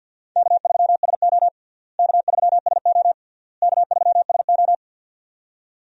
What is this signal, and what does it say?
Morse code, heard as a keyed tone, about 46 wpm, 700 Hz. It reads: K3SO K3SO K3SO
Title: USA Call Signs - 4 Characters - Repeated 3x - 50wpm